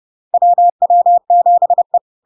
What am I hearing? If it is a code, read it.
Morse code, heard as a keyed tone, about 30 wpm, 700 Hz. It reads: WW7E
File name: LARK-FD-2022-Log-in-CW-30wpm-700-Hz